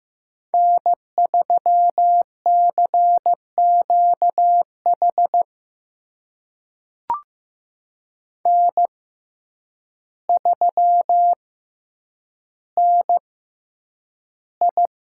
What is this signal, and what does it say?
Morse code, heard as a keyed tone, about 15 wpm, spaced as for 8 wpm, 700 Hz. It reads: N3CQH N3NI
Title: USA Call Signs - 6x Character Spacing - 15wpm